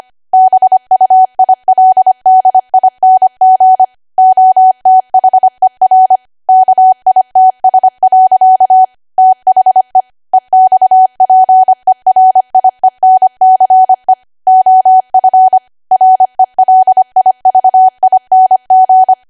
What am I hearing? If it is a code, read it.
Morse code, heard as a keyed tone, about 25 wpm, 750 Hz. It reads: BUILDING OTHER KITS. THE EXPERIENCE OF RELIVING